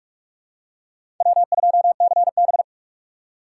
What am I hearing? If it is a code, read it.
Morse code, heard as a keyed tone, about 48 wpm, 700 Hz. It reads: W2CB
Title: USA Call Signs - 3x Character Spacing - 45wpm